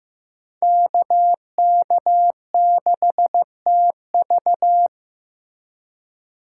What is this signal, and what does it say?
Morse code, heard as a keyed tone, about 15 wpm, 700 Hz. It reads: KK6TV
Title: K1USN SST QSO Practice - 15wpm